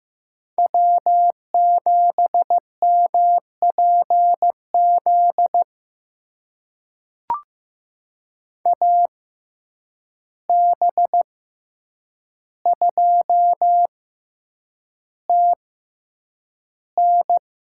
Morse code, heard as a keyed tone, about 15 wpm, spaced as for 8 wpm, 700 Hz. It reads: W7MPZ AB2TN
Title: USA Call Signs - 5 Characters - 6x Character Spacing - 15wpm